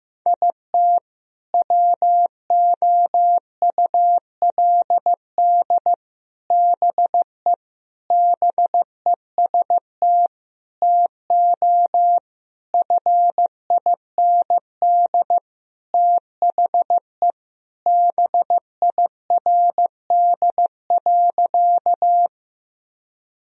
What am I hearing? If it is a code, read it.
Morse code, heard as a keyed tone, about 16 wpm, 700 Hz. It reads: IT WOULD BE BEST TO FIND THE BIRD.
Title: Sentences from Top 1000 Words - Review Part 2 - 15wpm